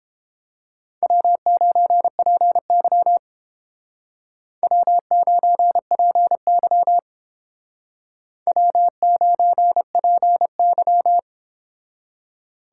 Morse code, heard as a keyed tone, about 31 wpm, 700 Hz. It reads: W9PY W9PY W9PY